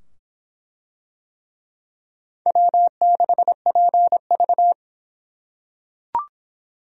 Morse code, heard as a keyed tone, about 24 wpm, 700 Hz. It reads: W6PV